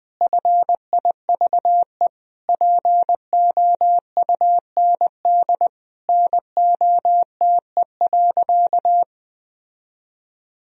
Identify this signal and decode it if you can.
Morse code, heard as a keyed tone, about 21 wpm, 700 Hz. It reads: FIVE POUND NOTE.
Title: Sentences from Top 1000 Words - Review Part 2 - 20wpm